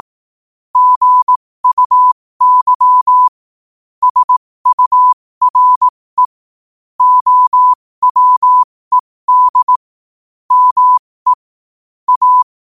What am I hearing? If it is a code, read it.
Morse code, heard as a keyed tone, about 18 wpm, spaced as for 15 wpm, 1000 Hz. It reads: GUY SURE OWED ME A
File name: qotdfiles / 15wpm